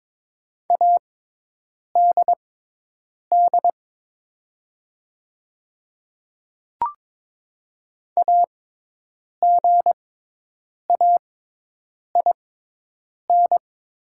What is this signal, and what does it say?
Morse code, heard as a keyed tone, about 22 wpm, spaced as for 8 wpm, 700 Hz. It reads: ADD AGAIN
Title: 6x Character Spacing - Top 200 Words - 22wpm